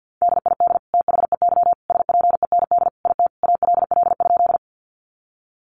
Morse code, heard as a keyed tone, about 50 wpm, 700 Hz. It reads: DID THEY SPEND IT ALL?
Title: Sentences from Top 800 Words 50wpm